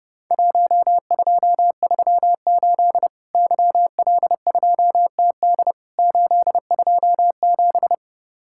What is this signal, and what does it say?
Morse code, heard as a keyed tone, about 32 wpm, 700 Hz. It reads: 1238 YL2TD 827
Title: Worked All Europe DX Contest - QTC Exchange - 30wpm